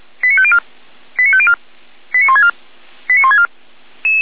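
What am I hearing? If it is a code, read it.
Five-tone selective call (ZVEI), heard as quick runs of five tones, tones of about 70 ms. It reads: 89504 89504 89164 89164
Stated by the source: ZVEI five-tone selective call 89249 received on SDR.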